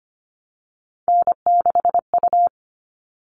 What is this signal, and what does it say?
Morse code, heard as a keyed tone, about 25 wpm, 700 Hz. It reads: N6U
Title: All Cut Numbers - 3-Digit 25wpm